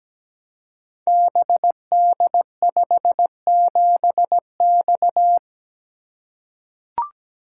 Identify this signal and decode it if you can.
Morse code, heard as a keyed tone, about 17 wpm, 700 Hz. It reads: BD57X
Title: Code Groups - 5 Letter:Numbers - 17wpm